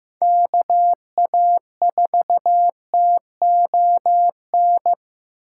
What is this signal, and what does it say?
Morse code, heard as a keyed tone, about 15 wpm, 700 Hz. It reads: KA4TON
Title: USA Call Signs - 6 Characters - 2x Character Spacing - 15wpm